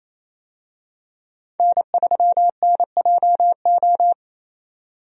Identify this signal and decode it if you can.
Morse code, heard as a keyed tone, about 28 wpm, 700 Hz. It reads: N3NJO
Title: USA Call Signs - 5 Characters - 28wpm